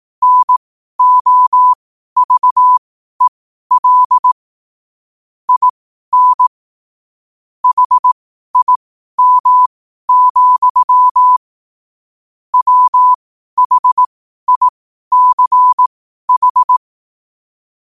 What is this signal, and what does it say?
Morse code, heard as a keyed tone, about 18 wpm, spaced as for 12 wpm, 1000 Hz. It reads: NOVEL IN HIM, WHICH
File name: qotdfiles / 12wpm